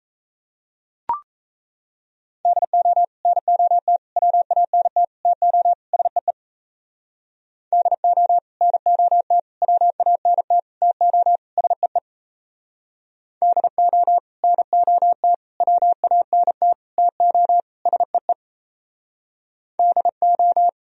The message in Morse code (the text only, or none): DO NOT WANT TO SEE DO NOT WANT TO SEE DO NOT WANT TO SEE DO